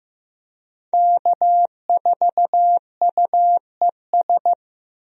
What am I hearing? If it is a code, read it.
Morse code, heard as a keyed tone, about 15 wpm, 700 Hz. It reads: K4UES
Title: USA Call Signs - 5 Characters - Repeated 3x - 15wpm